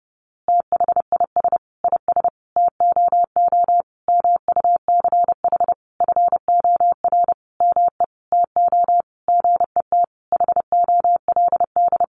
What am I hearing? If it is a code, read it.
Morse code, heard as a keyed tone, about 30 wpm, 700 Hz. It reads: THIS IS TOO MUCH FOR ME TO GET HOLD
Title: A Sound Of Thunder 30wpm